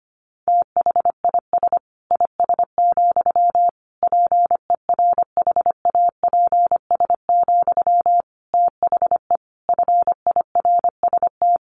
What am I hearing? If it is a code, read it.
Morse code, heard as a keyed tone, about 25 wpm, 700 Hz. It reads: THIS IS, PERHAPS, THE FIRST
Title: Sentences from Top 1500 Words 25wpm